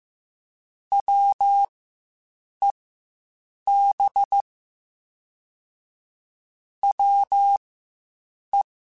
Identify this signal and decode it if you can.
Morse code, heard as a keyed tone, about 15 wpm, spaced as for 7 wpm, 800 Hz. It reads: WEB WE
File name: Quotes Spaced / Quotes Spaced-mp3s / Deceive sp @10